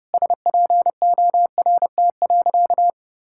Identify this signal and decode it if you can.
Morse code, heard as a keyed tone, about 30 wpm, 700 Hz. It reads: SPORT.
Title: Sentences from Top 1600 Words 30wpm